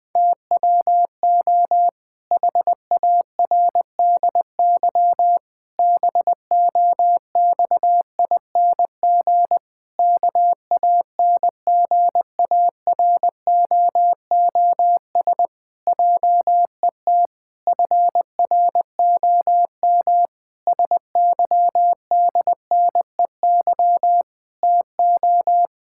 Morse code, heard as a keyed tone, about 20 wpm, 700 Hz. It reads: TWO HARDY BOXING KANGAROOS JET FROM SYDNEY TO
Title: Pangram Challenge v2 - 20wpm